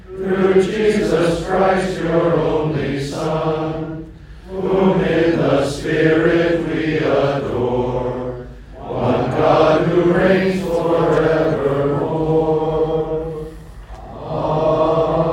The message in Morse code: none